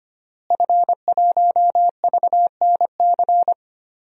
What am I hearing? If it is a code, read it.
Morse code, heard as a keyed tone, about 25 wpm, 700 Hz. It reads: F1VNC
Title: DXCC-100 Call Signs - 5 Characters - Repeated 3x - 25wpm